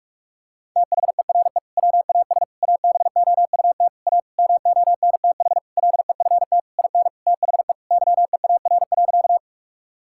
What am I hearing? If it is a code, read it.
Morse code, heard as a keyed tone, about 46 wpm, 700 Hz. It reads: THERE WAS ABOUT A MONTH LEFT IN THE YEAR.
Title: Sentences from Top 1000 Words - Review Part 5 - 45wpm